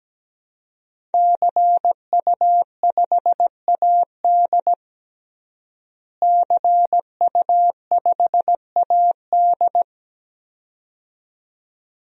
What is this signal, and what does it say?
Morse code, heard as a keyed tone, about 17 wpm, 700 Hz. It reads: CU5AD CU5AD
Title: DXCC-100 Call Signs - 5 Characters - Repeated 2x - 17wpm